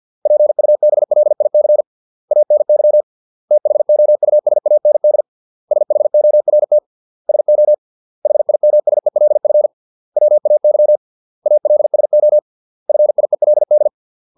Morse code, heard as a keyed tone, about 50 wpm, 600 Hz. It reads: PUBLIC ANY THOUSAND SHORT SO HIMSELF WAY ALSO FIELD